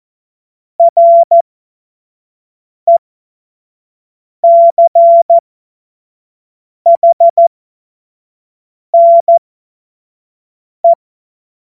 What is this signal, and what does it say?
Morse code, heard as a keyed tone, about 14 wpm, spaced as for 5 wpm, 680 Hz. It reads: RECHNE
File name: Morsekurs / Witze / Witz 14. Dezember 2021, 25 F70